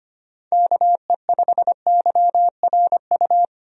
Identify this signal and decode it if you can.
Morse code, heard as a keyed tone, about 25 wpm, 700 Hz. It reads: KE5YRU
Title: USA Call Signs - 6 Characters - 25wpm